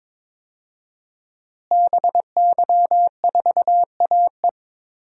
Morse code, heard as a keyed tone, about 22 wpm, 700 Hz. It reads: BY4AE
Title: DXCC-100 Call Signs - 5 Characters - Repeated 2x - 22wpm